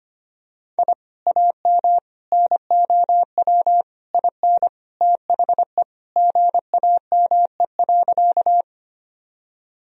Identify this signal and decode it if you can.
Morse code, heard as a keyed tone, about 25 wpm, 700 Hz. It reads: I AM NOW IN THE GAME.
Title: Sentences from Top 1000 Words - Review Part 3 - 25wpm